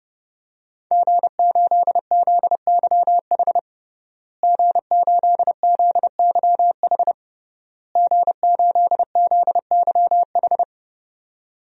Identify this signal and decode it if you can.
Morse code, heard as a keyed tone, about 30 wpm, 700 Hz. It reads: G8ZYH G8ZYH G8ZYH
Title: DXCC-100 Call Signs - 5 Characters - Repeated 3x - 30wpm